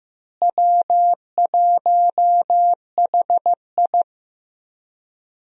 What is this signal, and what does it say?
Morse code, heard as a keyed tone, about 15 wpm, 700 Hz. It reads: W1HI
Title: USA Call Signs - 4 Characters - Repeated 3x - 15wpm